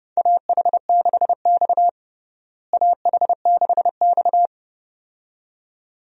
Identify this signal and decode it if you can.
Morse code, heard as a keyed tone, about 31 wpm, 700 Hz. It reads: AH6X AH6X
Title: USA Call Signs - 4 Characters - Repeated 2x - 30wpm